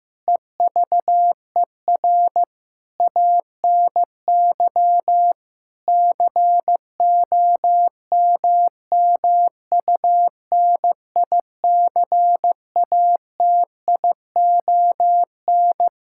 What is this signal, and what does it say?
Morse code, heard as a keyed tone, about 15 wpm, 700 Hz. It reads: EVER ANY COMMUNICATION
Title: Sentences from Top 1000 Words - Review Part 6 - 15wpm